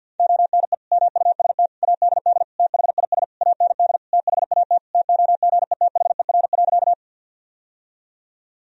Morse code, heard as a keyed tone, about 51 wpm, 700 Hz. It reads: ONE MUST ADD THIS AND THAT TOGETHER.
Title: Sentences from Top 1000 Words - Review Part 1 - 50wpm